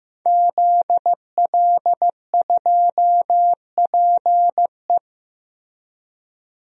Morse code, heard as a keyed tone, about 15 wpm, 700 Hz. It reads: ZL2PE
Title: DXCC-100 Call Signs - 5 Characters - Repeated 3x - 15wpm